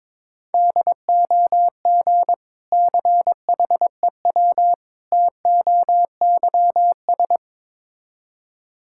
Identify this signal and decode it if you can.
Morse code, heard as a keyed tone, about 22 wpm, 700 Hz. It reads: DOG CHEW TOYS
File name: Difficult Words with Hints - 22wpm